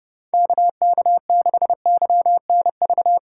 Morse code, heard as a keyed tone, about 28 wpm, 700 Hz. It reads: KK6YNV